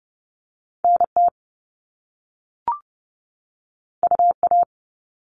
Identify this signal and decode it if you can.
Morse code, heard as a keyed tone, about 30 wpm, 700 Hz. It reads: NT UA
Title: Top Two Letter Combinations in Words 30wpm